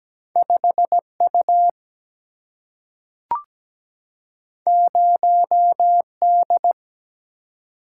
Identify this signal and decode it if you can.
Morse code, heard as a keyed tone, about 17 wpm, 700 Hz. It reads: 5 U 0 D
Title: Single-Letter Single-Number 17wpm